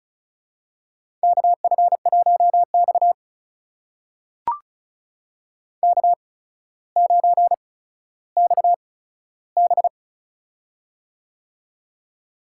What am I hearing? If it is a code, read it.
Morse code, heard as a keyed tone, about 36 wpm, spaced as for 15 wpm, 700 Hz. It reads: KF1X K9XB
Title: USA Call Signs - 4 Characters - 8x Character Spacing - 35wpm